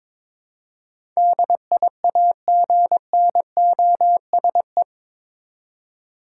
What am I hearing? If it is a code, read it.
Morse code, heard as a keyed tone, about 22 wpm, 700 Hz. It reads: DIAGNOSE